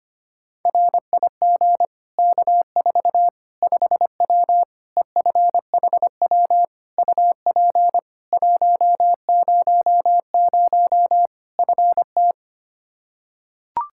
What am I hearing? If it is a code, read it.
Morse code, heard as a keyed tone, about 25 wpm, 700 Hz. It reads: RIG K4 5W EFHW UP 100 FT
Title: Practice QSO - Exchange 2 - Info - 25wpm